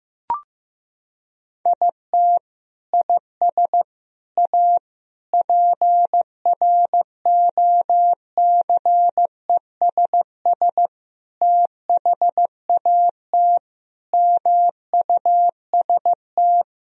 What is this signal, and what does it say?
Morse code, heard as a keyed tone, about 15 wpm, 700 Hz. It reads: IT IS A PROCESS THAT MUST
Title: Sentences from Top 1000 Words - Review Part 3 - 15wpm